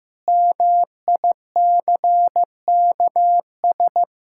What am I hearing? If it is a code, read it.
Morse code, heard as a keyed tone, about 15 wpm, 700 Hz. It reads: MICKS
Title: Pangram Challenge v2 - 15wpm